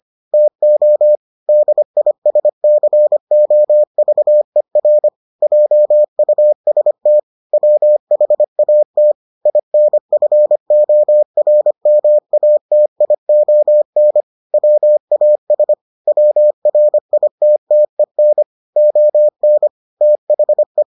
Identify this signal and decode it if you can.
Morse code, heard as a keyed tone, about 25 wpm, 600 Hz. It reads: TO DISCOVER JUST WHAT INFORMATION WAS WRITTEN ON THE